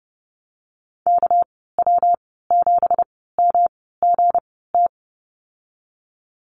Morse code, heard as a keyed tone, about 30 wpm, spaced as for 17 wpm, 700 Hz. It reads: KW7MGT
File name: Lesson 37g - Letter x - Call Signs 30wpm